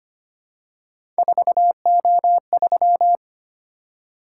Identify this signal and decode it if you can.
Morse code, heard as a keyed tone, about 25 wpm, 700 Hz. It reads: 4O3
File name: Three Letter-Number Combinations - 25wpm